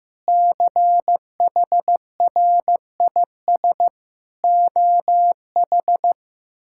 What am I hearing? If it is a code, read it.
Morse code, heard as a keyed tone, about 15 wpm, 700 Hz. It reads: CHRIS OH
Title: K1USN SST QSO Practice - 15wpm